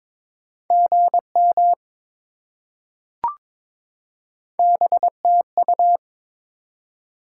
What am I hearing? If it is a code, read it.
Morse code, heard as a keyed tone, about 22 wpm, 700 Hz. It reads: GM BTU
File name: practice / 22wpm / Top 20 QSO Elements - 22wpm